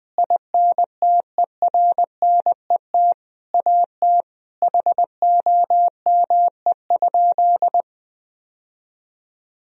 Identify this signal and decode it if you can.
Morse code, heard as a keyed tone, about 20 wpm, 700 Hz. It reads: INTERNET AT HOME?
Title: Sentences from Top 1000 Words - Review Part 4 - 20wpm